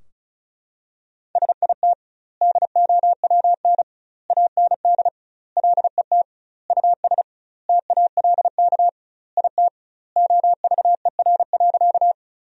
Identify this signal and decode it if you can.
Morse code, heard as a keyed tone, about 34 wpm, 700 Hz. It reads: SIT DOWN AND LET US TALK IT OVER.
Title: 2x Word-Spacing - Sentences from Top 500 Words - 35wpm